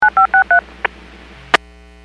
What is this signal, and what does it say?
DTMF, heard as telephone touch tones, tones of about 110 ms. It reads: 9563
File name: WPS / 03183 10-21 WPS 451.25 WPS